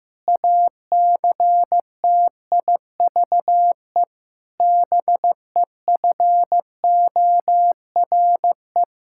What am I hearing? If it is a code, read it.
Morse code, heard as a keyed tone, about 15 wpm, 700 Hz. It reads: ACTIVE BEFORE
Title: Sentences from Top 1000 Words - Review Part 6 - 15wpm